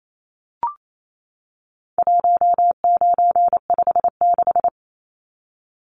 Morse code, heard as a keyed tone, about 28 wpm, 700 Hz. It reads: 1956
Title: Numbers - 4-Digit 28wpm